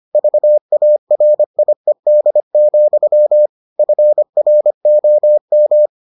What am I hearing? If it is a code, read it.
Morse code, heard as a keyed tone, about 25 wpm, 600 Hz. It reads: VARIED, FROM